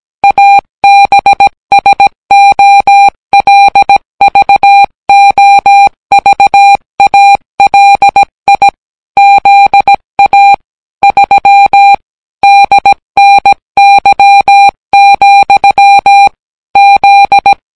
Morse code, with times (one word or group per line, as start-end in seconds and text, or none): 0.2-8.7 ABSOLVOVALI
9.2-10.5 ZA
11.0-11.9 3
12.4-16.3 DNY,
16.8-17.5 Z